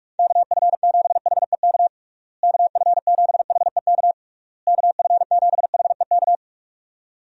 Morse code, heard as a keyed tone, about 46 wpm, 700 Hz. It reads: KF7HEK KF7HEK KF7HEK
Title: USA Call Signs - 6 Characters - Repeated 3x - 45wpm